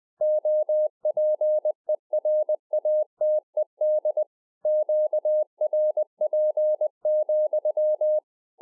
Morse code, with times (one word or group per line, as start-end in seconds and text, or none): 0.2-4.2 OPERATED
4.7-8.2 QRP,